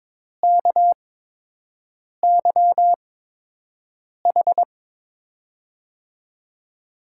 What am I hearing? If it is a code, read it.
Morse code, heard as a keyed tone, about 22 wpm, spaced as for 6 wpm, 700 Hz. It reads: KYH
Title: USA Call Signs - 6 Characters - 8x Character Spacing - 22wpm